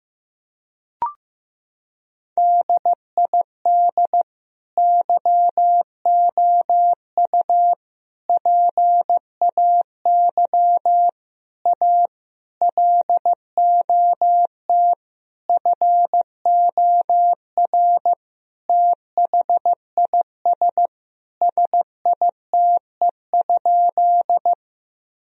Morse code, heard as a keyed tone, about 15 wpm, 700 Hz. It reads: DID YOU PAY A LOT FOR THIS SITE?
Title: Sentences from Top 1000 Words - Review Part 4 - 15wpm